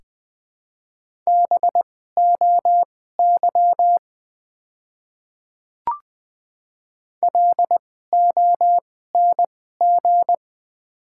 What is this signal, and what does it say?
Morse code, heard as a keyed tone, about 20 wpm, spaced as for 14 wpm, 700 Hz. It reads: BOY LONG
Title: 2x Character Spacing - Top 300 Words - 20wpm